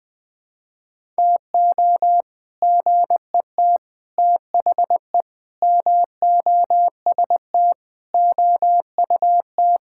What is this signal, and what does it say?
Morse code, heard as a keyed tone, about 20 wpm, 700 Hz. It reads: TO GET THE MOST OUT
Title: Sentences with 7-Word Phrases - 20wpm